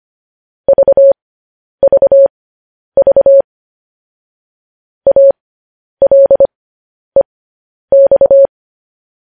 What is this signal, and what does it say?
Morse code, heard as a keyed tone, about 25 wpm, spaced as for 10 wpm, 550 Hz. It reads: VVV ALEX
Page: news-headlines-in-morse-code-at-10-wpm